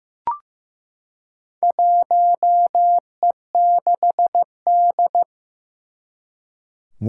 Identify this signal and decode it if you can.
Morse code, heard as a keyed tone, about 15 wpm, 700 Hz. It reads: 1E6D
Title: Four Challenging Letter-Number Combinations 15wpm